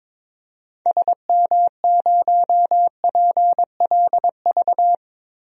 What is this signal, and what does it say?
Morse code, heard as a keyed tone, about 22 wpm, 700 Hz. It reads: SM0PLV
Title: DXCC-100 Call Signs - 6 Characters - Repeated 2x - 22wpm